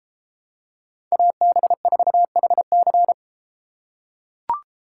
Morse code, heard as a keyed tone, about 30 wpm, 700 Hz. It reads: AB4HC